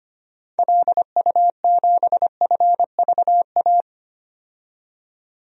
Morse code, heard as a keyed tone, about 25 wpm, 690 Hz. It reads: LU7FVA
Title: DXCC-100 Call Signs - 6 Characters - 25wpm